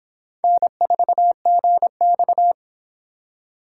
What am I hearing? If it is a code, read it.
Morse code, heard as a keyed tone, about 26 wpm, 700 Hz. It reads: N4GX